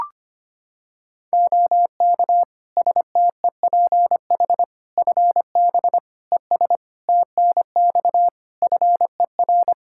Morse code, heard as a keyed tone, about 25 wpm, 700 Hz. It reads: OK STEPH FB ES TNX FER
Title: Practice QSO - Exchange 2 - Info - 25wpm